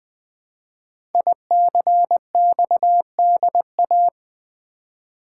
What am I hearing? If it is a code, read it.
Morse code, heard as a keyed tone, about 20 wpm, 700 Hz. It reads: ICXDA
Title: Code Groups - 5 Letter:Numbers - 20wpm